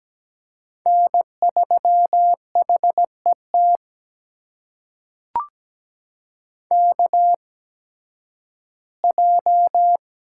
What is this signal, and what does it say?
Morse code, heard as a keyed tone, about 17 wpm, spaced as for 11 wpm, 700 Hz. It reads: N3HET KJ
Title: USA Call Signs - 5 Characters - 8x Character Spacing - 17wpm